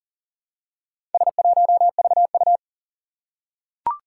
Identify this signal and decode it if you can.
Morse code, heard as a keyed tone, about 40 wpm, 700 Hz. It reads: S1VU
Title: Four Challenging Letter-Number Combinations 40wpm